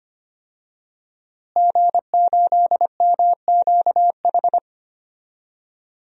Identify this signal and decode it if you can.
Morse code, heard as a keyed tone, about 25 wpm, 700 Hz. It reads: G8MQH
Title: DXCC-100 Call Signs - 5 Characters - Repeated 3x - 25wpm